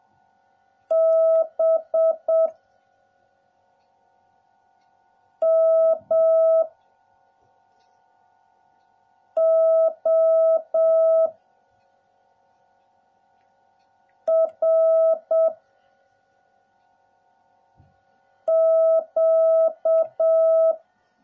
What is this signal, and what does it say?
Morse code, heard as a keyed tone, about 7 wpm, 660 Hz. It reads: B M O R Q